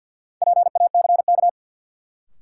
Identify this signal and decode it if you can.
Morse code, heard as a keyed tone, about 49 wpm, 700 Hz. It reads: PACK
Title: Sentences from Top 1300 Words - New Word Drill Alt - 50wpm